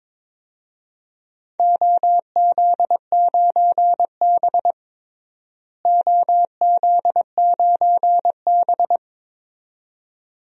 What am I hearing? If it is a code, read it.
Morse code, heard as a keyed tone, about 22 wpm, 700 Hz. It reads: OZ9B OZ9B
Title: DXCC-100 Call Signs - 4 Characters - Repeated 2x - 22wpm